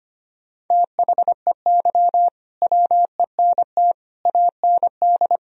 Morse code, heard as a keyed tone, about 25 wpm, 700 Hz. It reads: THEY WENT AND
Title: Sentences from Top 1000 Words - Review Part 3 - 25wpm